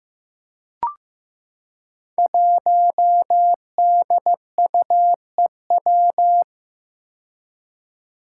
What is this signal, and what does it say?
Morse code, heard as a keyed tone, about 15 wpm, 700 Hz. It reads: 1DUEW
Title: Five Challenging Letter-Number Combinations 15wpm